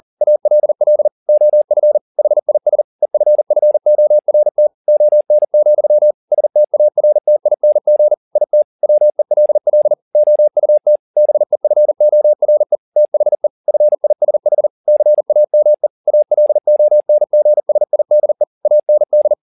Morse code, heard as a keyed tone, about 40 wpm, 600 Hz. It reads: ALL OF HIS EFFORT ON, STARTING IT WELL OUT BEFORE THE FISH CAME ALONGSIDE AND
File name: the old man and the sea 2 40wpm